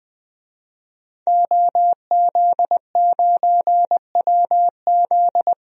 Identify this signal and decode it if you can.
Morse code, heard as a keyed tone, about 20 wpm, 700 Hz. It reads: OZ9WZ
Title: DXCC-100 Call Signs - 5 Characters - 20wpm